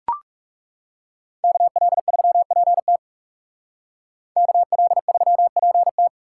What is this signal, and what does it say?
Morse code, heard as a keyed tone, about 37 wpm, 700 Hz. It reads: KL3PT KL3PT
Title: Speed Racing - USA 2x2 Call Signs 30wpm